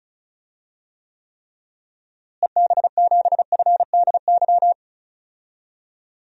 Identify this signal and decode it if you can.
Morse code, heard as a keyed tone, about 36 wpm, 700 Hz. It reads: EB7FDY
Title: DXCC-100 Call Signs - 6 Characters - Repeated 3x - 35wpm